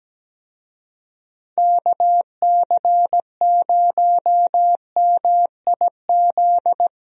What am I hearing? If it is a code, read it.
Morse code, heard as a keyed tone, about 17 wpm, 700 Hz. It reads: KC0MIZ
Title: USA Call Signs - 6 Characters - Repeated 2x - 17wpm